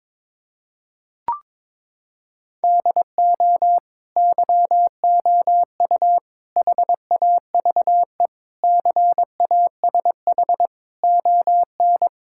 Morse code, heard as a keyed tone, about 21 wpm, 700 Hz. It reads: DO YOU HAVE CASH ON
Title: Sentences from Top 1000 Words - Review Part 5 - 22wpm